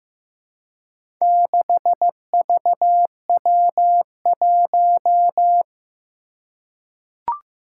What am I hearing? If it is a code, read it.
Morse code, heard as a keyed tone, about 15 wpm, 700 Hz. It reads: 6VW1
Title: Four Challenging Letter-Number Combinations 15wpm